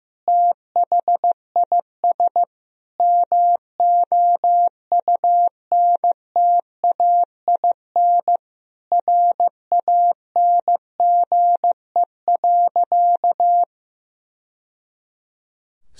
Morse code, heard as a keyed tone, about 15 wpm, 700 Hz. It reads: THIS MOUNTAIN RANGE.
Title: Sentences from Top 1000 Words - Review Part 4 - 15wpm